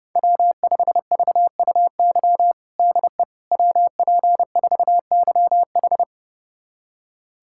Morse code, heard as a keyed tone, about 30 wpm, 700 Hz. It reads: W5VUY DE WP4YH
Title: USA Callsign DE Callsign - 5 Characters - 30wpm